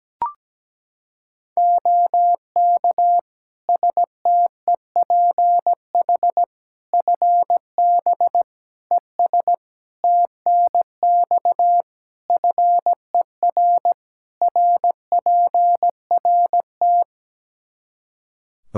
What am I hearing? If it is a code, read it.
Morse code, heard as a keyed tone, about 17 wpm, 700 Hz. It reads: OK STEPH FB ES TNX FER RPRT
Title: Practice QSO - Exchange 2 - Info - 17wpm